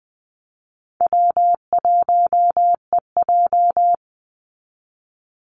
Morse code, heard as a keyed tone, about 20 wpm, 700 Hz. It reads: W1EJ
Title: NA SOTA Activator Callsigns 20wpm